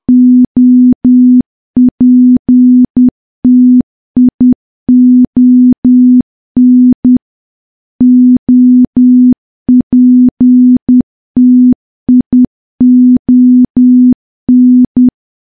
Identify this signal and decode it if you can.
Morse code, heard as a keyed tone, about 10 wpm, 250 Hz. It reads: OPTION OPTION
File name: morse / morse-training-10WPM